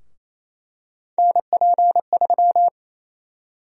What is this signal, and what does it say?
Morse code, heard as a keyed tone, about 28 wpm, 700 Hz. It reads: NP3
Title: USA Call Sign Prefixes - Repeated 2x - 28wpm